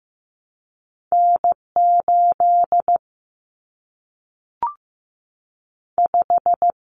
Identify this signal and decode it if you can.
Morse code, heard as a keyed tone, about 15 wpm, 700 Hz. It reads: N 8 5
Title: Common Cut Numbers - 2-Digit 15wpm